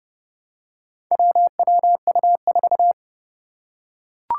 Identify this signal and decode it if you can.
Morse code, heard as a keyed tone, about 29 wpm, 690 Hz. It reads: WWU4
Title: Four Challenging Letter-Number Combinations 30wpm